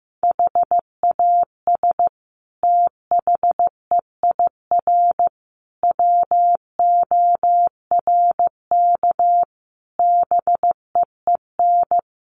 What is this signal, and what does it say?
Morse code, heard as a keyed tone, about 15 wpm, 700 Hz. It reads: HAS THEIR WORK BEEN
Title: Sentences from Top 1500 Words 15wpm